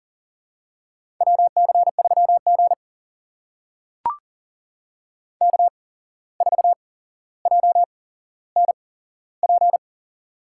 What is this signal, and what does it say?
Morse code, heard as a keyed tone, about 40 wpm, spaced as for 16 wpm, 700 Hz. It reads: WC3G K4JNP
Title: USA Call Signs - 8x Character Spacing - 40wpm